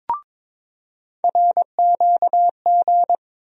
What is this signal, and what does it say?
Morse code, heard as a keyed tone, about 22 wpm, 700 Hz. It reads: RQG
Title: USA Call Sign Postfixes - 3 Letters 22wpm